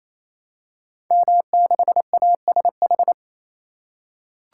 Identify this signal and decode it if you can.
Morse code, heard as a keyed tone, about 27 wpm, 700 Hz. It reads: M6ASH
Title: Code Groups - 5 Letter:Numbers - 28wpm